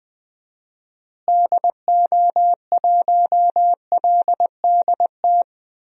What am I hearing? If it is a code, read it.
Morse code, heard as a keyed tone, about 20 wpm, 700 Hz. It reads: DO1LDT
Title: DXCC-100 Call Signs - 6 Characters - Repeated 2x - 20wpm